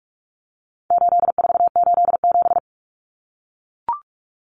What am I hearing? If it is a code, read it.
Morse code, heard as a keyed tone, about 44 wpm, 700 Hz. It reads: 8487
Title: Common Cut Numbers - 4-Digit 45wpm